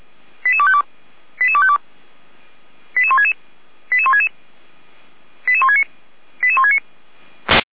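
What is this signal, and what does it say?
Five-tone selective call (ZVEI), heard as quick runs of five tones, tones of about 70 ms. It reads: 88252 88252 88166 88166 88169 88169
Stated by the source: ZVEI five-tone selective call received on SDR.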